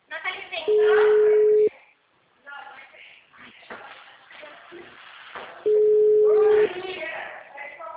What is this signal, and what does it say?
Ringback tone (Europe, 425 Hz), with speech.